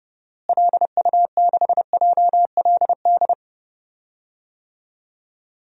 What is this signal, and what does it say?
Morse code, heard as a keyed tone, about 32 wpm, 700 Hz. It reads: LU6JLD
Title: DXCC-100 Call Signs - 6 Characters - Repeated 3x - 30wpm